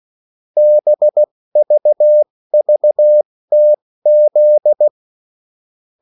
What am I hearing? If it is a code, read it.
Morse code, heard as a keyed tone, about 16 wpm, spaced as for 14 wpm, 600 Hz. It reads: BVVTZ
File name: morse mp3 / revision9-vitesse 14 mots